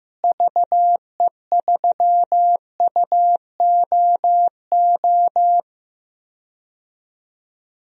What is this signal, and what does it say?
Morse code, heard as a keyed tone, about 15 wpm, 700 Hz. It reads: VE3UOO
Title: DXCC-100 Call Signs - 6 Characters - Repeated 3x - 15wpm